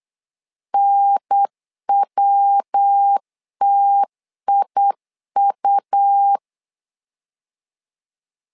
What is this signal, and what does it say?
Morse code, heard as a keyed tone, about 8 wpm, 800 Hz. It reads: NWTIU
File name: LEZIONI / 3 MWRO / 3 MWR 8wpm 2444